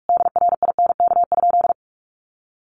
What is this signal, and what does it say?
Morse code, heard as a keyed tone, about 44 wpm, 700 Hz. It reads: DRINK?
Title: Sentences from Top 1500 Words 45wpm